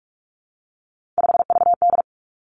Morse code, heard as a keyed tone, about 46 wpm, 700 Hz. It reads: 5VD
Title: Three Challenging Letter-Number Combinations 45wpm